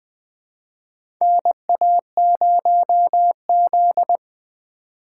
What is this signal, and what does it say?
Morse code, heard as a keyed tone, about 20 wpm, 700 Hz. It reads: NA0Z
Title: USA Call Signs - 4 Characters - 2x Character Spacing - 20wpm